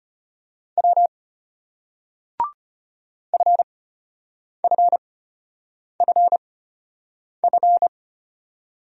Morse code, heard as a keyed tone, about 34 wpm, spaced as for 8 wpm, 700 Hz. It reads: W FFFF